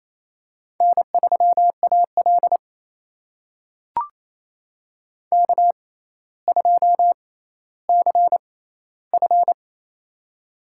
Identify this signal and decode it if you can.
Morse code, heard as a keyed tone, about 27 wpm, spaced as for 14 wpm, 700 Hz. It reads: N3AL K2CF
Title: USA Call Signs - 4 Characters - 6x Character Spacing - 28wpm